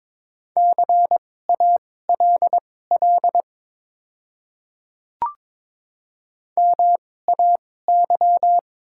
Morse code, heard as a keyed tone, about 22 wpm, spaced as for 16 wpm, 700 Hz. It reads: CALL MAY
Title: 2x Character Spacing - Top 100 Words - 22wpm